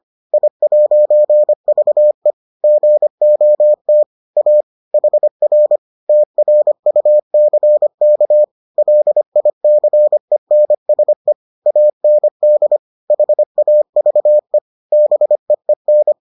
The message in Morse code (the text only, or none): I'VE GOT A HR TRUCK LICENSE AND HAVE BEEN